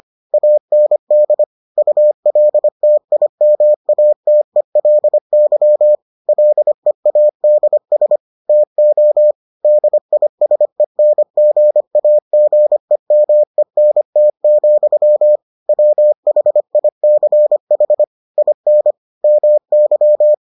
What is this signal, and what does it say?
Morse code, heard as a keyed tone, about 25 wpm, 600 Hz. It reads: AND ULTIMATELY LEADS TO DISENGAGEMENT, WHICH IN MY